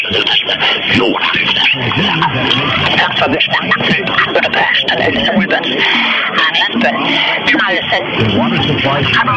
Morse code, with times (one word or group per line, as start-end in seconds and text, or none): none